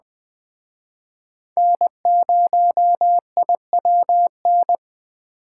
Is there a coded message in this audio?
Yes: Morse code from 1.6 to 4.8 s, heard as a keyed tone, about 20 wpm, 700 Hz. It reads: N0IWN